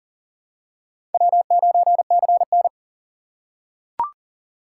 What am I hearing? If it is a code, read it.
Morse code, heard as a keyed tone, about 40 wpm, 700 Hz. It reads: W9CN
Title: USA Call Signs - 6x Character Spacing - 40wpm